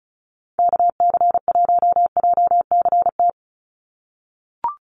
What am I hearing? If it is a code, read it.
Morse code, heard as a keyed tone, about 34 wpm, 700 Hz. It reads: KC1JCT
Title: USA Call Signs 35wpm